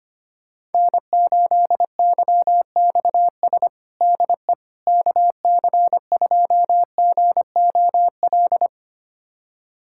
Morse code, heard as a keyed tone, about 25 wpm, 700 Hz. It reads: N8YXS DE KC2GOL
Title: USA Callsign DE Callsign 25wpm